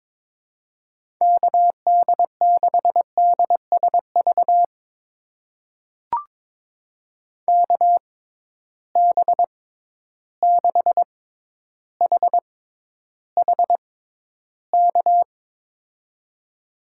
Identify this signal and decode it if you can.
Morse code, heard as a keyed tone, about 22 wpm, spaced as for 11 wpm, 700 Hz. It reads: KD6DSV KB6HHK
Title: USA Call Signs - 6 Characters - 6x Character Spacing - 22wpm